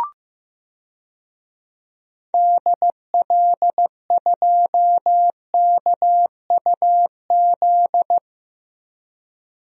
Morse code, heard as a keyed tone, about 15 wpm, 700 Hz. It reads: DL2KUZ
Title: DXCC-100 Call Signs - 6 Characters - 15wpm